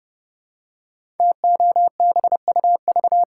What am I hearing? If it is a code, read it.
Morse code, heard as a keyed tone, about 30 wpm, 700 Hz. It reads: TOBUV
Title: Code Groups - 5 Letters - 30wpm